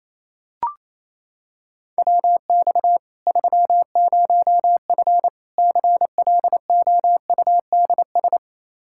Morse code, heard as a keyed tone, about 28 wpm, 700 Hz. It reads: WX 30F CLOUDS
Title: Practice QSO - Exchange 2 - Info - 28wpm